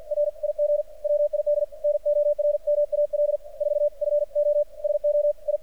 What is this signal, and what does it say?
Morse code, heard as a keyed tone, about 49 wpm, 590 Hz, with radio noise. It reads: RIG WID NORMAL 4 FO SO I
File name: archived / CRSnet